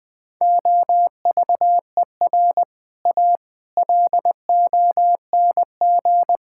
Morse code, heard as a keyed tone, about 19 wpm, 700 Hz. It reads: OVER A LONG
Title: Sets of 6 Words - 20wpm